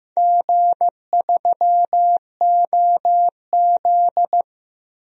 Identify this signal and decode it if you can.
Morse code, heard as a keyed tone, about 16 wpm, 700 Hz. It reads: G3OZ
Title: DXCC-100 Call Signs - 4 Characters - Repeated 2x - 15wpm